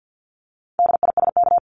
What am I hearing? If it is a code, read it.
Morse code, heard as a keyed tone, about 50 wpm, 700 Hz. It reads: DISK
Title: Top 751-1000 Words 50wpm